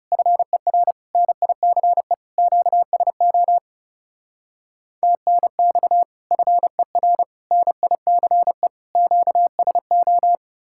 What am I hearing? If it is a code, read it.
Morse code, heard as a keyed tone, about 39 wpm, 700 Hz. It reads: FER NICE QSO TNX FER NICE QSO